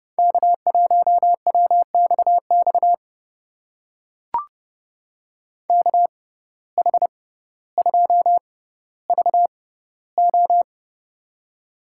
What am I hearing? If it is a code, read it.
Morse code, heard as a keyed tone, about 32 wpm, spaced as for 16 wpm, 700 Hz. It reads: K1WXX KH2VO
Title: USA Call Signs - 5 Characters - 6x Character Spacing - 30wpm